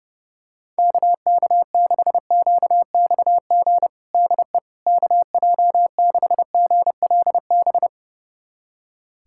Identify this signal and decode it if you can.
Morse code, heard as a keyed tone, about 30 wpm, 700 Hz. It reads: KK6QXG DE KJ6GLB
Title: USA Callsign DE Callsign - 6 Characters - 30wpm